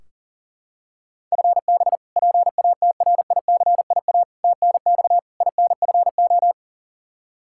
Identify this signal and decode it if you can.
Morse code, heard as a keyed tone, about 40 wpm, 700 Hz. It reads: FB PATRICIA TNX INFO
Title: Practice QSO - Exchange 3 - More Info - 40wpm